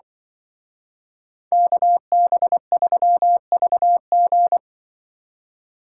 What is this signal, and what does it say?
Morse code, heard as a keyed tone, about 25 wpm, 700 Hz. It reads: KB3VG
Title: Speed Racing - USA 2x2 Call Signs 20wpm